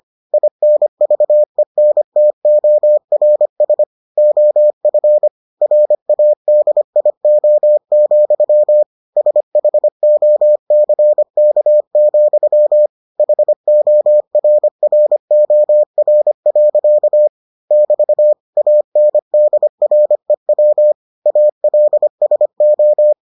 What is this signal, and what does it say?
Morse code, heard as a keyed tone, about 25 wpm, 600 Hz. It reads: INVENTORS OF RADIO, SHOCK, HORROR. <BT> ANDREW ALSO